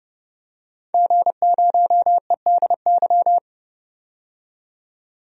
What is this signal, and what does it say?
Morse code, heard as a keyed tone, about 30 wpm, 700 Hz. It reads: G0EDY
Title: DXCC-100 Call Signs - 5 Characters - 30wpm